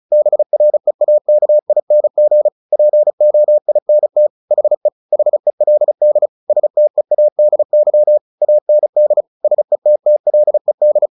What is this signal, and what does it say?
Morse code, heard as a keyed tone, about 34 wpm, 600 Hz. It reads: BREAKING POINT HE HELD STEADY AND SETTLED
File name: the old man and the sea / the old man and the sea 1 35wpm